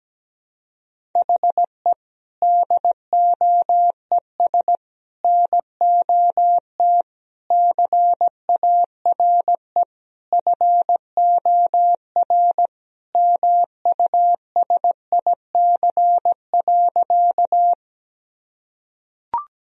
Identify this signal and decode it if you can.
Morse code, heard as a keyed tone, about 17 wpm, 700 Hz. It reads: HE DOES NOT CARE FOR MUSIC.
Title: Sentences from Top 1000 Words - Review Part 1 - 17wpm